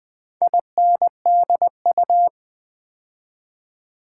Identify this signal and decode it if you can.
Morse code, heard as a keyed tone, about 20 wpm, 700 Hz. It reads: INDU
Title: Build A Word - 4 Letters At A Time - 10-Letter Words - 20wpm